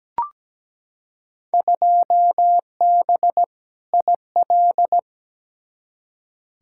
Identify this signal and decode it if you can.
Morse code, heard as a keyed tone, about 17 wpm, 700 Hz. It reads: 2B IL
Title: Field Day Exchanges Encore - 17wpm